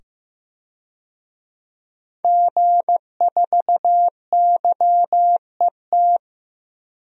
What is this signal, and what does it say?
Morse code, heard as a keyed tone, about 15 wpm, 700 Hz. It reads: G4YET
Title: DXCC-100 Call Signs - 5 Characters - 15wpm